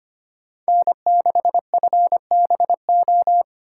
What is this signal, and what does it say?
Morse code, heard as a keyed tone, about 25 wpm, 700 Hz. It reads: N6FBO
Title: USA Call Signs - 4x Character Spacing - 25wpm